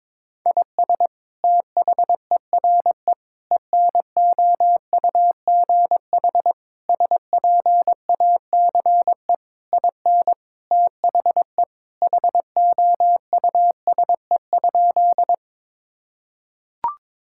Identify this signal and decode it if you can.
Morse code, heard as a keyed tone, about 22 wpm, 700 Hz. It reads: IS THERE ENOUGH SPACE IN THE HOUSE?
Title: Sentences from Top 1000 Words - Review Part 5 - 22wpm